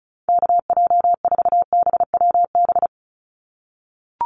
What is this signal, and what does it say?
Morse code, heard as a keyed tone, about 34 wpm, 700 Hz. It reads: KJ4BWB
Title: USA Call Signs 35wpm